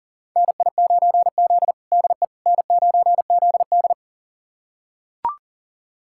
Morse code, heard as a keyed tone, about 40 wpm, 700 Hz. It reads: NI9Z DE N9ZD
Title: USA Callsign DE Callsign - 4 Characters - 40wpm